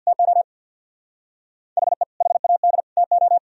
Morse code, heard as a keyed tone, about 63 wpm, 700 Hz. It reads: TO HE HAD TO